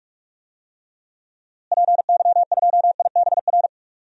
Morse code, heard as a keyed tone, about 43 wpm, 700 Hz. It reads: PY2IBR
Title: DXCC-100 Call Signs - 6 Characters - Repeated 2x - 45wpm